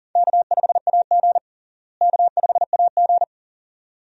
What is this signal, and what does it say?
Morse code, heard as a keyed tone, about 40 wpm, 700 Hz. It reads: K5AG K5AG
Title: USA Call Signs - 4 Characters - Repeated 2x - 40wpm